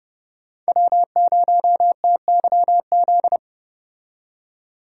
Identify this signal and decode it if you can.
Morse code, heard as a keyed tone, about 35 wpm, 700 Hz. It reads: W0TYZ